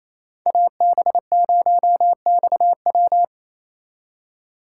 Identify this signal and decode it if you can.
Morse code, heard as a keyed tone, about 35 wpm, 700 Hz. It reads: AB0XW
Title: Speed Racing - USA 2x2 Call Signs 28wpm